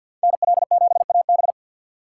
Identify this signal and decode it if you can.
Morse code, heard as a keyed tone, about 48 wpm, 700 Hz. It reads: N L 7 A B
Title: USA Call Signs - 5 Characters - 6x Character Spacing - 50wpm